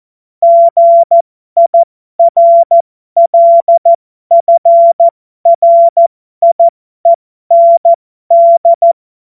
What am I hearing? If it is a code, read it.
Morse code, heard as a keyed tone, about 14 wpm, spaced as for 12 wpm, 680 Hz. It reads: GIRLFRIEND
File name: Morsekurs / Witze